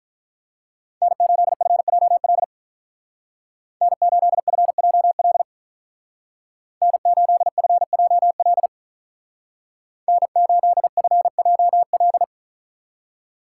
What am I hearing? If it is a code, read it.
Morse code, heard as a keyed tone, about 43 wpm, 700 Hz. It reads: N8FJL N8FJL N8FJL N8FJL
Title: Speed Racing - USA 1x3 Call Signs 35wpm